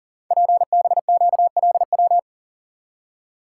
Morse code, heard as a keyed tone, about 42 wpm, 700 Hz. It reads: PBQLW
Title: Code Groups - 5 Letters - 40wpm